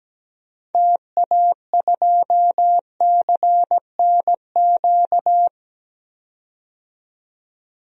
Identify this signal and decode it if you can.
Morse code, heard as a keyed tone, about 17 wpm, 700 Hz. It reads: TA2CNQ
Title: DXCC-100 Call Signs - 6 Characters - 17wpm